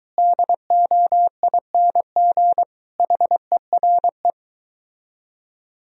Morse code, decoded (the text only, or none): DOING HERE